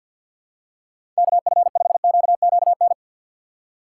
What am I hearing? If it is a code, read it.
Morse code, heard as a keyed tone, about 53 wpm, 700 Hz. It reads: KF5QQN
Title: USA Call Signs - 6 Characters - 8x Character Spacing - 50wpm